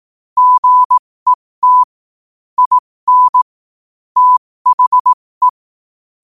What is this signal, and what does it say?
Morse code, heard as a keyed tone, about 18 wpm, spaced as for 15 wpm, 1000 Hz. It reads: GET IN THE
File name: qotdfiles / 15wpm